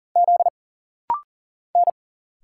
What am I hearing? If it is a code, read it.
Morse code, heard as a keyed tone, about 40 wpm, 700 Hz. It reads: Z N
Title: Single Letters - Rapid-Fire 40wpm